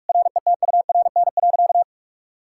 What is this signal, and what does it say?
Morse code, heard as a keyed tone, about 41 wpm, 700 Hz. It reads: RETURN.
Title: Sentences from Top 1000 Words - Review Part 6 - 45wpm